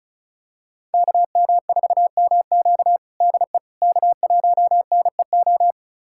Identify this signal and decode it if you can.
Morse code, heard as a keyed tone, about 36 wpm, 700 Hz. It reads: KM4MQ DE K1NEO
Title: USA Callsign DE Callsign - 5 Characters - 35wpm